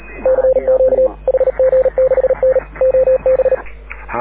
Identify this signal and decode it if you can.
Morse code, heard as a keyed tone, about 39 wpm, 530 Hz. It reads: CQ HG6N OB
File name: AUDIO2019 / HG6N CW SSB MIX